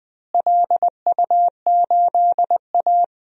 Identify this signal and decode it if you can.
Morse code, heard as a keyed tone, about 21 wpm, 700 Hz. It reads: LU8A
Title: DXCC-100 Call Signs - 4 Characters - Repeated 3x - 20wpm